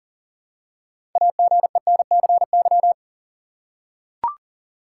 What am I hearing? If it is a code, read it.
Morse code, heard as a keyed tone, about 37 wpm, 700 Hz. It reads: AGENCY
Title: Six Letter Words - 40wpm